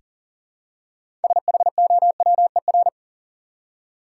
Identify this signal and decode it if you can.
Morse code, heard as a keyed tone, about 40 wpm, 700 Hz. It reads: SHOWER
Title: Six Letter Words - 40wpm